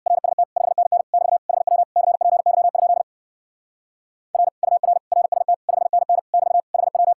Morse code, heard as a keyed tone, about 66 wpm, 700 Hz. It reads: RST 5NN <BT> HW CPY? R UR RST 5NN <BT> HW